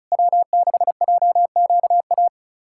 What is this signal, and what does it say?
Morse code, heard as a keyed tone, about 36 wpm, 690 Hz. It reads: W6JQA
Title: SOTA QSO Practice 35wpm